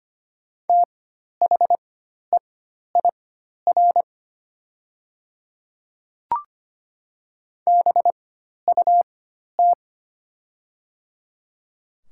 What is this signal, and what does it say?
Morse code, heard as a keyed tone, about 25 wpm, spaced as for 12 wpm, 700 Hz. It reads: THEIR BUT
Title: 4x Character Spacing - Top 100 Words - 25wpm